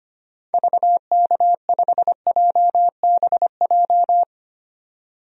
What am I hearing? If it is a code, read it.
Morse code, heard as a keyed tone, about 25 wpm, 700 Hz. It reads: VK5JBJ
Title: DXCC-100 Call Signs - 6 Characters - 25wpm